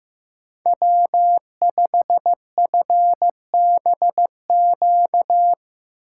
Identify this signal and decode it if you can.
Morse code, heard as a keyed tone, about 16 wpm, 700 Hz. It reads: W5FBQ
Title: Speed Racing - USA 1x3 Call Signs 15wpm